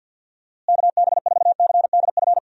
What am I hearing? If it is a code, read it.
Morse code, heard as a keyed tone, about 52 wpm, 700 Hz. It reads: KB4CDF
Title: USA Call Signs - 6 Characters - Repeated 2x - 50wpm